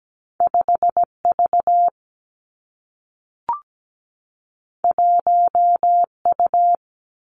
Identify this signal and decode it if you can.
Morse code, heard as a keyed tone, about 17 wpm, 700 Hz. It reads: 5V 1U
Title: Two Challenging Letter-Number Combinations 17wpm